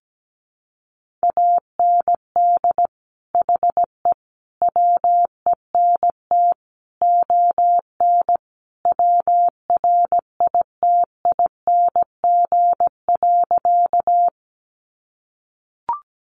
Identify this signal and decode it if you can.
Morse code, heard as a keyed tone, about 17 wpm, 700 Hz. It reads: AND HE WENT ON WRITING.
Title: Sentences from Top 800 Words 17wpm